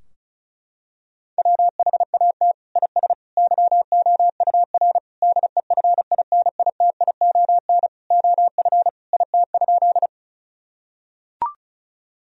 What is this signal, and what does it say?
Morse code, heard as a keyed tone, about 34 wpm, 700 Hz. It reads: WHAT IS YOUR DEFINITION OF IT?
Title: Sentences from Top 1000 Words - Review Part 4 - 35wpm